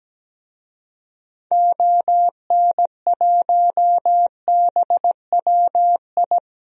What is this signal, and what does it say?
Morse code, heard as a keyed tone, about 17 wpm, 700 Hz. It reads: ON1BWI
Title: DXCC-100 Call Signs - 6 Characters - 17wpm